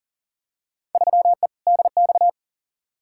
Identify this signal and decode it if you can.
Morse code, heard as a keyed tone, about 37 wpm, 700 Hz. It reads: 3E DX
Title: Field Day Exchanges Encore - 40wpm